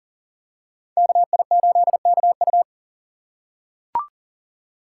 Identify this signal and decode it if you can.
Morse code, heard as a keyed tone, about 40 wpm, 700 Hz. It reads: KI8KU
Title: USA Call Signs - 5 Characters - 4x Character Spacing - 40wpm